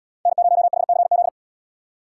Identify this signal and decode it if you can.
Morse code, heard as a keyed tone, about 77 wpm, 700 Hz. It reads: N0HCZ